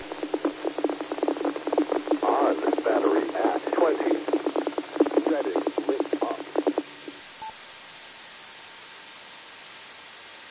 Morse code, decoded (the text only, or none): none